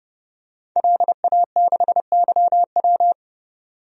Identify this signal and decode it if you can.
Morse code, heard as a keyed tone, about 32 wpm, 700 Hz. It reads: LA6YW
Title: DXCC-100 Call Signs - 5 Characters - 30wpm